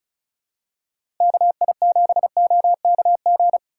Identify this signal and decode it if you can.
Morse code, heard as a keyed tone, about 36 wpm, 700 Hz. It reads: KI7OKG
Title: USA Call Signs - 6 Characters - 4x Character Spacing - 35wpm